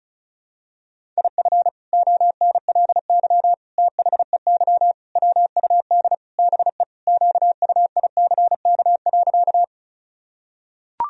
Morse code, heard as a keyed tone, about 34 wpm, 700 Hz. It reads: IF ONLY THEY WUD BE QUICK.
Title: Sentences from Top 500 Words and Abbreviations 35wpm